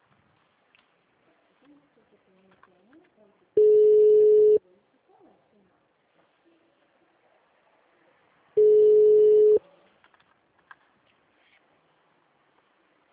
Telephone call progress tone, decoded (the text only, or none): ringback tone